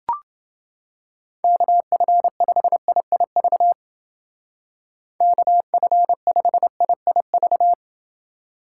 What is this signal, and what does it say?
Morse code, heard as a keyed tone, about 29 wpm, 700 Hz. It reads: KF5IIV KF5IIV